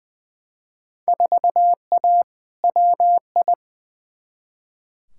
Morse code, heard as a keyed tone, about 21 wpm, 700 Hz. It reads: 4A WI
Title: Field Day Exchanges - 20wpm